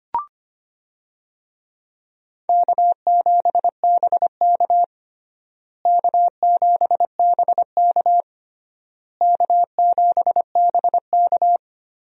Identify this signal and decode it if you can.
Morse code, heard as a keyed tone, about 25 wpm, 700 Hz. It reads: K7BK K7BK K7BK
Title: USA Call Signs - 4 Characters - Repeated 3x - 25wpm